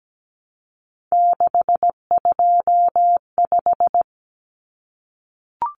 Morse code, heard as a keyed tone, about 17 wpm, 700 Hz. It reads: 625
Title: Numbers - 3-Digit 17wpm